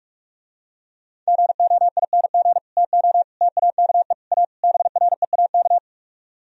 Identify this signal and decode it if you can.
Morse code, heard as a keyed tone, about 44 wpm, 700 Hz. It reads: GOING TO TAKE A BREAK